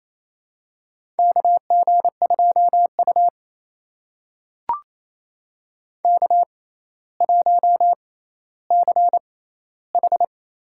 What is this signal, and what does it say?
Morse code, heard as a keyed tone, about 27 wpm, spaced as for 14 wpm, 700 Hz. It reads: KG2U K1CH
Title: USA Call Signs - 4 Characters - 6x Character Spacing - 28wpm